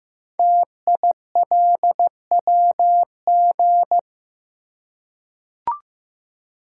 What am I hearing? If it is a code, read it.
Morse code, heard as a keyed tone, about 15 wpm, 700 Hz. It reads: TILWG
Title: Code Groups - 5 Letters - 15wpm